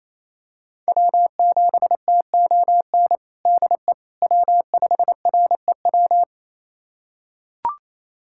Morse code, heard as a keyed tone, about 28 wpm, 700 Hz. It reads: W7TON DE W5REW
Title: USA Callsign DE Callsign - 5 Characters - 28wpm